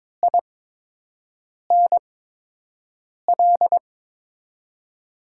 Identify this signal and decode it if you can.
Morse code, heard as a keyed tone, about 22 wpm, spaced as for 6 wpm, 700 Hz. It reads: INL
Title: USA Call Signs - 8x Character Spacing - 22wpm